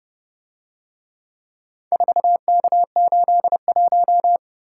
Morse code, heard as a keyed tone, about 29 wpm, 700 Hz. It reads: 4K81